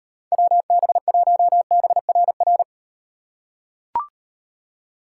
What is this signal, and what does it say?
Morse code, heard as a keyed tone, about 38 wpm, 700 Hz. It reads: WB1BRR